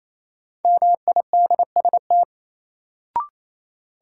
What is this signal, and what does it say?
Morse code, heard as a keyed tone, about 28 wpm, 700 Hz. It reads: MIDST
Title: Sets of 4 Words - New Word Drill - 28wpm